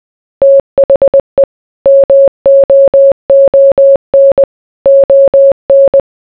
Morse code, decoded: THE MOON ON